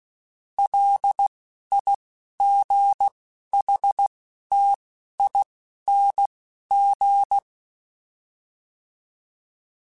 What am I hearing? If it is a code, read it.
Morse code, heard as a keyed tone, about 16 wpm, 800 Hz. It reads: L I G H T I N G